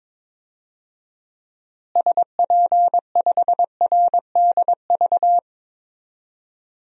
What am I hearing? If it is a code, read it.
Morse code, heard as a keyed tone, about 22 wpm, 700 Hz. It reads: SP5RDV
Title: DXCC-100 Call Signs - 6 Characters - Repeated 3x - 22wpm